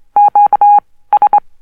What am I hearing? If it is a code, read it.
Morse code, heard as a keyed tone, about 23 wpm, spaced as for 16 wpm, 820 Hz. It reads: QS
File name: morse code oscillator - low pitched